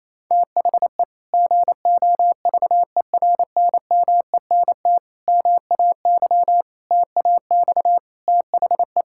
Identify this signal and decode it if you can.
Morse code, heard as a keyed tone, about 28 wpm, 700 Hz. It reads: THE GOVERNMENT MAY TAX THE
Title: Sentences from Top 1000 Words - Review Part 5 - 28wpm